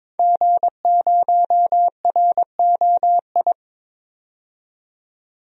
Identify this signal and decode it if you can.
Morse code, heard as a keyed tone, about 22 wpm, 700 Hz. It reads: G0ROI
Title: DXCC-100 Call Signs - 5 Characters - 22wpm